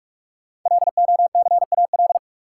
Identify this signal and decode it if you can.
Morse code, heard as a keyed tone, about 45 wpm, 700 Hz. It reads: LOCAL
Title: Sentences from Top 1500 Words - New Word Drill Alt - 45wpm